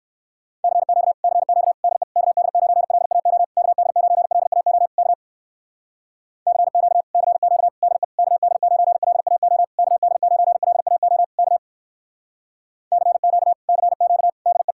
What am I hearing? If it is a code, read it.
Morse code, heard as a keyed tone, about 59 wpm, 700 Hz. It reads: CQ CQ DE KD9LAO KD9LAO K CQ CQ DE KD9LAO KD9LAO K CQ CQ DE